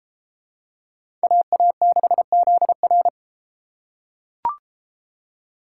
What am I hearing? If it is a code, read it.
Morse code, heard as a keyed tone, about 30 wpm, 690 Hz. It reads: AA6ZR